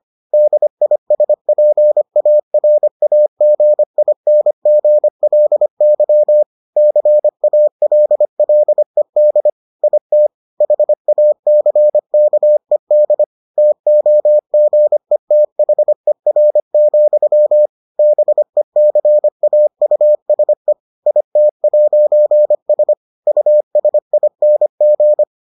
Morse code, heard as a keyed tone, about 25 wpm, 600 Hz. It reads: DISPARAGINGLY CALLED IT HACKED TOGETHER, BECAUSE IT'S USING